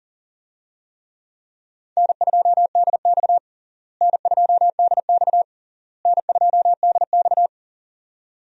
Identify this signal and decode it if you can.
Morse code, heard as a keyed tone, about 40 wpm, 700 Hz. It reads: N2DX N2DX N2DX
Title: USA Call Signs - 4 Characters - Repeated 3x - 40wpm